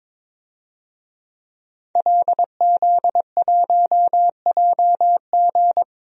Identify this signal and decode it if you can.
Morse code, heard as a keyed tone, about 22 wpm, 700 Hz. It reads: LZ1JG
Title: DXCC-100 Call Signs - 5 Characters - Repeated 2x - 22wpm